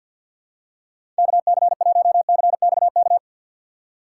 Morse code, heard as a keyed tone, about 52 wpm, 700 Hz. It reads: KC1CXK
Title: USA Call Signs - 6 Characters - 6x Character Spacing - 50wpm